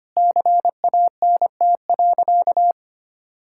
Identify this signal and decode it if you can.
Morse code, heard as a keyed tone, about 25 wpm, 700 Hz. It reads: CANT.
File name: practice / 25wpm / Sentences with 2-Word Phrases - 25wpm